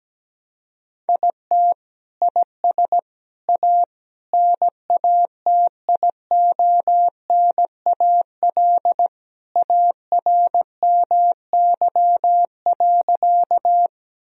Morse code, heard as a keyed tone, about 17 wpm, 700 Hz. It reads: IT IS A NATIONAL ARMY.
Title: Sentences from Top 1000 Words - Review Part 4 - 17wpm